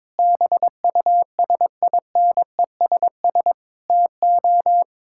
Morse code, heard as a keyed tone, about 22 wpm, 700 Hz. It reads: BUSINESS TO
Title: Sentences from Top 1000 Words - Review Part 6 - 22wpm